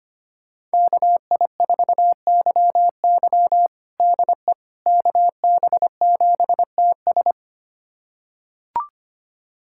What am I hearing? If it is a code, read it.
Morse code, heard as a keyed tone, about 25 wpm, 700 Hz. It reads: KI4YY DE KB7TS
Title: USA Callsign DE Callsign - 5 Characters - 25wpm